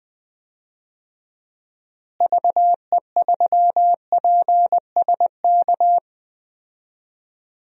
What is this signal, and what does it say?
Morse code, heard as a keyed tone, about 20 wpm, 700 Hz. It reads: VE3PSK
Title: DXCC-100 Call Signs - 6 Characters - Repeated 3x - 20wpm